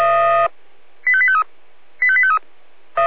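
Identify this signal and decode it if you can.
Five-tone selective call (ZVEI), heard as quick runs of five tones, tones of about 70 ms. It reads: 85853 85853
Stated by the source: Five-tone ZVEI selective call received by SDR on 496.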